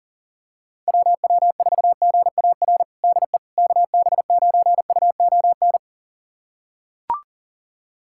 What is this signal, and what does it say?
Morse code, heard as a keyed tone, about 40 wpm, 700 Hz. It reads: WW4GAR DE KB9UON
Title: USA Callsign DE Callsign - 6 Characters - 40wpm